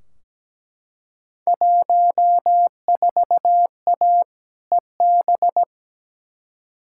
Morse code, heard as a keyed tone, about 17 wpm, 700 Hz. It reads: 14A EB
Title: Field Day Exchanges Encore - 17wpm